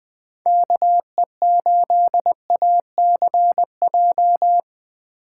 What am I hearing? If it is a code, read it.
Morse code, heard as a keyed tone, about 20 wpm, 700 Hz. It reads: KE8ACJ
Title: USA Call Signs - 6 Characters - 2x Character Spacing - 20wpm